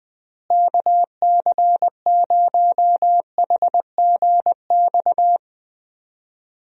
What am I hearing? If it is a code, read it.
Morse code, heard as a keyed tone, about 20 wpm, 700 Hz. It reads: KC0HGX
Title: USA Call Signs - 6 Characters - Repeated 3x - 20wpm